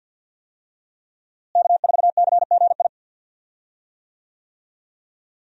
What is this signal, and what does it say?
Morse code, heard as a keyed tone, about 52 wpm, 700 Hz. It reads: K4CGI
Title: USA Call Signs - 5 Characters - Repeated 3x - 50wpm